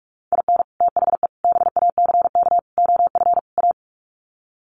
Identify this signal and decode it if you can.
Morse code, heard as a keyed tone, about 44 wpm, 700 Hz. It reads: IN THE BACK OF A
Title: Sets of 5 Words Encore2 45wpm